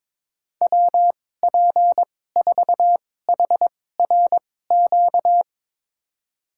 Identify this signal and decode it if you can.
Morse code, heard as a keyed tone, about 22 wpm, spaced as for 16 wpm, 700 Hz. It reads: WP4HRQ
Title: USA Call Signs - 6 Characters - 2x Character Spacing - 22wpm